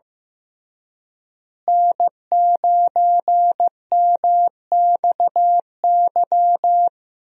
Morse code, heard as a keyed tone, about 15 wpm, 700 Hz. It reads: N9MXY
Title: USA Call Signs - 5 Characters - Repeated 3x - 15wpm